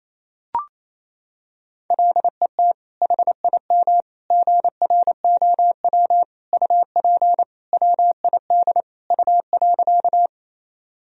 Morse code, heard as a keyed tone, about 28 wpm, 700 Hz. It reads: LET HIM GROW UP WID U.
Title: Sentences from Top 300 Words and Abbreviations 28wpm